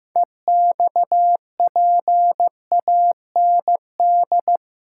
Morse code, heard as a keyed tone, about 15 wpm, 700 Hz. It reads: EXPAND
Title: Sentences from Top 1400 Words - New Word Drill Alt - 15wpm